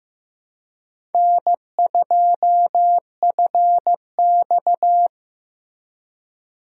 Morse code, heard as a keyed tone, about 15 wpm, 700 Hz. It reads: N2FX
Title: USA Call Signs - 4 Characters - Repeated 2x - 15wpm